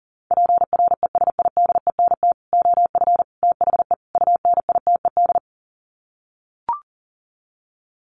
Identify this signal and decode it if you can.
Morse code, heard as a keyed tone, about 40 wpm, 700 Hz. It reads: PRESIDENT OF THE UNITED
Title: Sets of 4 Words 40wpm